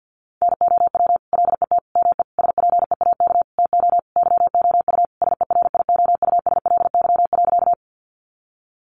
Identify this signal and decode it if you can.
Morse code, heard as a keyed tone, about 48 wpm, 700 Hz. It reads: NOW LET ME SPEAK TO YOU SERIOUSLY.
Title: Sentences from Top 1200 Words 50wpm